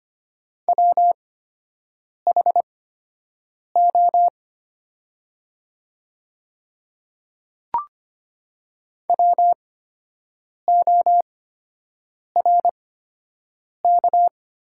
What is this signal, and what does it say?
Morse code, heard as a keyed tone, about 25 wpm, spaced as for 7 wpm, 700 Hz. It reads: WHO WORK
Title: 8x Character Spacing - Top 300 Words - 25wpm